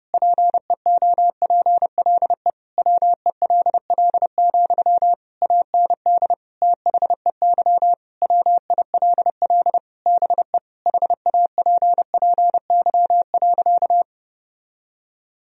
Morse code, heard as a keyed tone, about 30 wpm, 700 Hz. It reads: PEOPLE WELL, AND THEY WILL BE HAPPY.
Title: Sentences from Top 1000 Words - Review Part 2 - 30wpm